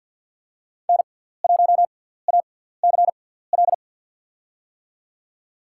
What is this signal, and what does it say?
Morse code, heard as a keyed tone, about 54 wpm, spaced as for 17 wpm, 700 Hz. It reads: N1ACL
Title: USA Call Signs - 5 Characters - 6x Character Spacing - 50wpm